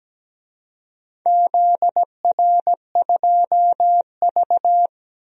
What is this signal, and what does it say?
Morse code, heard as a keyed tone, about 17 wpm, 700 Hz. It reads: ZR2V
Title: DXCC-100 Call Signs - 4 Characters - 17wpm